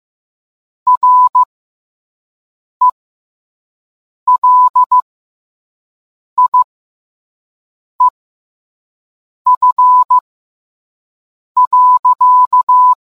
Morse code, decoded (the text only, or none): RELIEF.